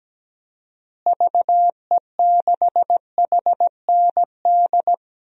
Morse code, heard as a keyed tone, about 17 wpm, 700 Hz. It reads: VE6HND
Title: DXCC-100 Call Signs - 6 Characters - Repeated 2x - 17wpm